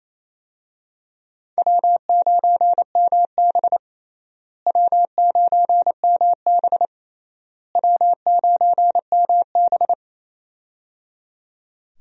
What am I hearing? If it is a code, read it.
Morse code, heard as a keyed tone, about 28 wpm, 700 Hz. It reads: W9MB W9MB W9MB
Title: USA Call Signs - 4 Characters - Repeated 3x - 28wpm